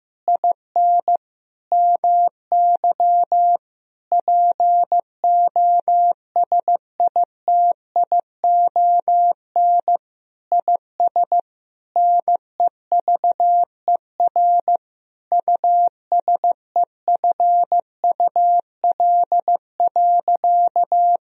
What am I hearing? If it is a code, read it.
Morse code, heard as a keyed tone, about 15 wpm, 700 Hz. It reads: IN MY POSITION IS NEVER USEFUL.
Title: Sentences from Top 1000 Words - Review Part 4 - 15wpm